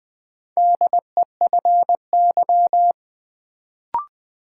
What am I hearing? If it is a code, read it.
Morse code, heard as a keyed tone, about 20 wpm, 700 Hz. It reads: DEFY
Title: Build A Word - 2 Letters At A Time - 4-Letter Words - 20wpm